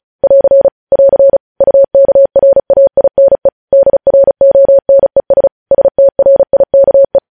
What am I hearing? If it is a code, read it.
Morse code, heard as a keyed tone, about 36 wpm, 550 Hz. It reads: <AR> <AR> UKRAINE DRONES STRIKE
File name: podcast / News headlines-1x-35x35